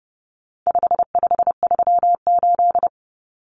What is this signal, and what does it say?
Morse code, heard as a keyed tone, about 30 wpm, 700 Hz. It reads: 5538
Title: Common Cut Numbers - 4-Digit 30wpm